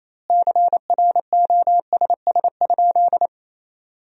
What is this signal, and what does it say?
Morse code, heard as a keyed tone, about 28 wpm, 700 Hz. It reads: CROSS?
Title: 2x Word-Spacing - Sentences from Top 300 Words - 28wpm